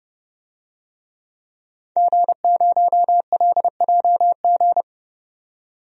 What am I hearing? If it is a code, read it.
Morse code, heard as a keyed tone, about 30 wpm, 700 Hz. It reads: G0LJG
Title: DXCC-100 Call Signs - 5 Characters - 30wpm